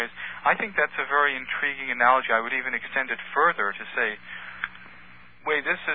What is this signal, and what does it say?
Emergency Alert System (SAME) tones (header bursts), heard as harsh buzzing bursts, with speech. It is heard from 0.6 to 4.9 s.